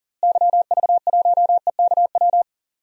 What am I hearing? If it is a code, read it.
Morse code, heard as a keyed tone, about 38 wpm, 700 Hz. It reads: YV1EKW
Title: DXCC-100 Call Signs - 6 Characters - Repeated 3x - 40wpm